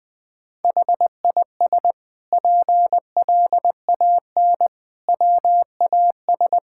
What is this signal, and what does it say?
Morse code, heard as a keyed tone, about 20 wpm, 700 Hz. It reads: HIS PLAN WAS
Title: Sentences from Top 1000 Words - Review Part 6 - 20wpm